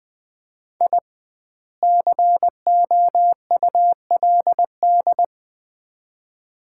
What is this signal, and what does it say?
Morse code, heard as a keyed tone, about 20 wpm, 700 Hz. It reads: I COULD
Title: 2x Word-Spacing - Sets of 2 Words - 20wpm